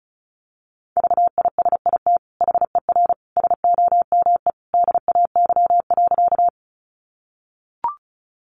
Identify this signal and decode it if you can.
Morse code, heard as a keyed tone, about 34 wpm, 700 Hz. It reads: VISIT HER SOME DAY.
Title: Sentences from Top 900 Words 35wpm